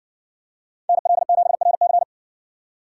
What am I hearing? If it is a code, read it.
Morse code, heard as a keyed tone, about 58 wpm, 700 Hz. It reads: NL7RC